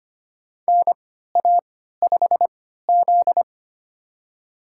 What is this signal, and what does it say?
Morse code, heard as a keyed tone, about 25 wpm, spaced as for 14 wpm, 700 Hz. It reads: NA5Z
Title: USA Call Signs - 4 Characters - 3x Character Spacing - 25wpm